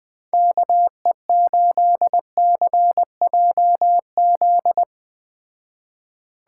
Morse code, heard as a keyed tone, about 20 wpm, 700 Hz. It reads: KE8CJZ